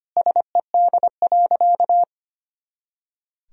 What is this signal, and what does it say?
Morse code, heard as a keyed tone, about 25 wpm, 700 Hz. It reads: SED.
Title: Sentences from Top 100 Words and Abbreviations 25wpm